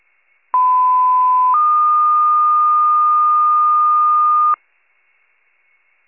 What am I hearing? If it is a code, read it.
Two-tone page, heard as a one-second tone, then a three-second tone, about 1010 then 1220 Hz. It is heard from 0.5 to 4.5 s.